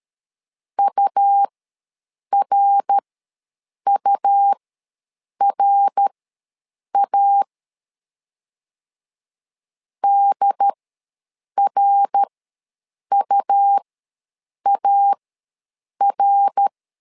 Morse code, heard as a keyed tone, about 13 wpm, spaced as for 7 wpm, 800 Hz. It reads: URURA DRUAR
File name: LEZIONI / 3 MWRO / 3 MWR ws 12wpm 1809